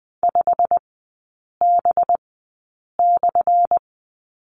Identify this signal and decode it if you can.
Morse code, heard as a keyed tone, about 20 wpm, spaced as for 8 wpm, 690 Hz. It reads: 5B/
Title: 205 Lesson 33e - Letter j - Three Characters 20wpm